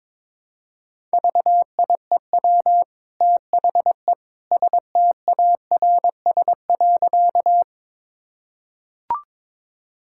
Morse code, heard as a keyed tone, about 22 wpm, 700 Hz. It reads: VIEW THE STARS.
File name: practice / 22wpm / Easy Sentences with Hints - 22wpm